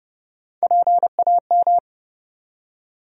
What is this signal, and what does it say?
Morse code, heard as a keyed tone, about 30 wpm, 700 Hz. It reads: PAM
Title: USA Common Names - Max 3 Letters - 30wpm